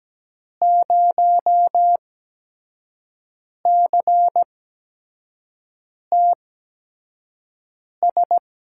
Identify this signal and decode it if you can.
Morse code, heard as a keyed tone, about 17 wpm, spaced as for 5 wpm, 700 Hz. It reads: 0CTS
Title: USA Call Signs - 6 Characters - 8x Character Spacing - 17wpm